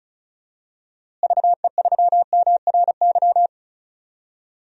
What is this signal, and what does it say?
Morse code, heard as a keyed tone, about 34 wpm, 700 Hz. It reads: VE3MRY
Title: DXCC-100 Call Signs - 6 Characters - Repeated 3x - 35wpm